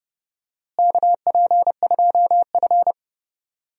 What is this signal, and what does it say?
Morse code, heard as a keyed tone, about 30 wpm, 700 Hz. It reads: KP2F
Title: DXCC-100 Call Signs - 4 Characters - Repeated 3x - 30wpm